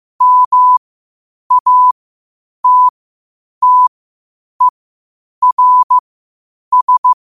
Morse code, heard as a keyed tone, about 15 wpm, spaced as for 8 wpm, 1000 Hz. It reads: MATTERS
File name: qotdfiles / 8wpm